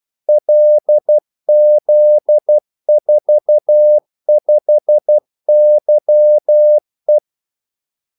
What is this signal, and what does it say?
Morse code, heard as a keyed tone, about 12 wpm, 600 Hz. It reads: LZ45YE
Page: 2 ekstra ordmellemrum mellem kaldesignalerne